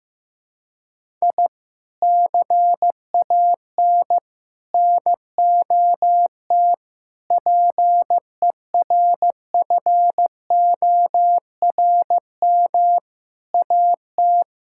Morse code, heard as a keyed tone, about 15 wpm, 700 Hz. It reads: I CAN NOT PERFORM AT
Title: Sentences from Top 1000 Words - Review Part 5 - 15wpm